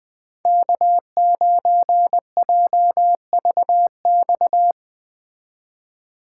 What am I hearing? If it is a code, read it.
Morse code, heard as a keyed tone, about 20 wpm, 700 Hz. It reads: K9JVX